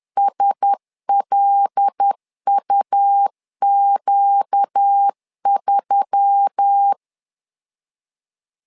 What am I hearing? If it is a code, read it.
Morse code, heard as a keyed tone, about 10 wpm, 800 Hz. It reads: SLUQ3
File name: LEZIONI / 5 LBVC / 5 L 10wpm 2259